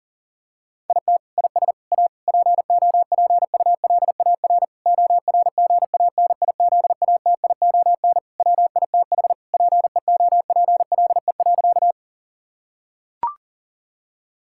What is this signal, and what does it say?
Morse code, heard as a keyed tone, about 40 wpm, 700 Hz. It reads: IT IS A POPULAR ORGANIZATION WITH PEOPLE.
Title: Sentences from Top 1000 Words - Review Part 5 - 40wpm